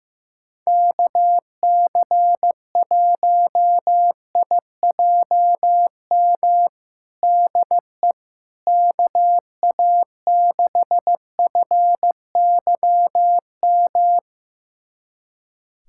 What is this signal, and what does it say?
Morse code, heard as a keyed tone, about 15 wpm, 700 Hz. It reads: KC1IJM DE KA6FYM
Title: USA Callsign DE Callsign - 6 Characters - 15wpm